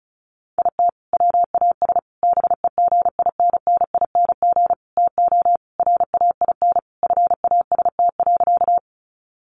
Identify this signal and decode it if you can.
Morse code, heard as a keyed tone, about 34 wpm, 700 Hz. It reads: IT WAS BEGINNING TO RAIN FAST.
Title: Sentences from Top 800 Words 35wpm